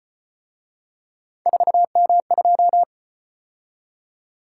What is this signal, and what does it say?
Morse code, heard as a keyed tone, about 33 wpm, 700 Hz. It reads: 4M2